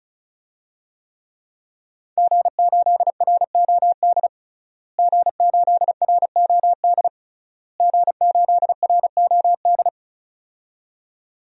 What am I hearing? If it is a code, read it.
Morse code, heard as a keyed tone, about 36 wpm, 700 Hz. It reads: G8ROD G8ROD G8ROD
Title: DXCC-100 Call Signs - 5 Characters - Repeated 3x - 35wpm